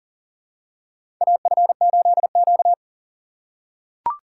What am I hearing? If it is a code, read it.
Morse code, heard as a keyed tone, about 40 wpm, 700 Hz. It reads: AF8Q
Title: USA Call Signs - 2x Character Spacing - 40wpm